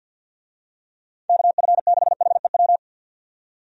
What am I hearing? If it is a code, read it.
Morse code, heard as a keyed tone, about 49 wpm, 700 Hz. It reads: KF6HEW
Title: USA Call Signs - 6 Characters - 50wpm